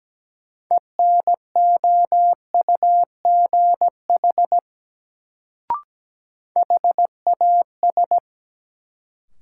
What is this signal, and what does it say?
Morse code, heard as a keyed tone, about 17 wpm, 700 Hz. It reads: ENOUGH HAS
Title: Sets of 2 Words - New Word Drill - 17wpm